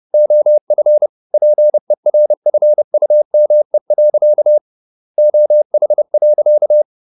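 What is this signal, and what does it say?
Morse code, heard as a keyed tone, about 30 wpm, 600 Hz. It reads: OF PERFUME. OH.
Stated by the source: Morse code at 30 wpm, the novel Never by Ken Follett in English, from CW med Gnister.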